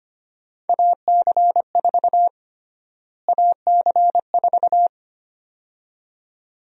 Morse code, heard as a keyed tone, about 25 wpm, 700 Hz. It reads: AC4 AC4
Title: USA Call Sign Prefixes - Repeated 2x - 25wpm